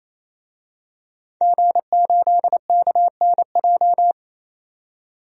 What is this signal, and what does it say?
Morse code, heard as a keyed tone, about 28 wpm, 700 Hz. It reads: G8KNJ
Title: DXCC-100 Call Signs - 5 Characters - 28wpm